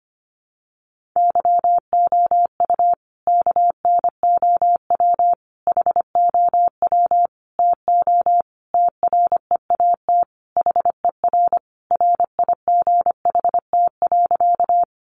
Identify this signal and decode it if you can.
Morse code, heard as a keyed tone, about 25 wpm, 700 Hz. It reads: YOU KNOW HOW TO TREAT HER RIGHT.
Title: Sentences from Top 1300 Words 25wpm